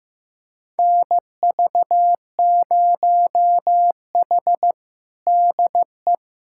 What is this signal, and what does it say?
Morse code, heard as a keyed tone, about 15 wpm, 700 Hz. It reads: NV0H DE
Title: USA Callsign DE Callsign - 4 Characters - 15wpm